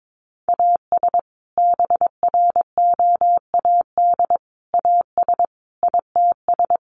Morse code, heard as a keyed tone, about 22 wpm, 700 Hz. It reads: AS BROAD AS ITS
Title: Sentences from Top 1000 Words 22wpm